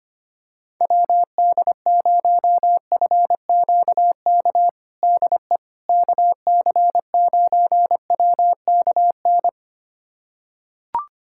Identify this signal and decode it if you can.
Morse code, heard as a keyed tone, about 25 wpm, 700 Hz. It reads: WD0FQK DE KC9WKN
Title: USA Callsign DE Callsign 25wpm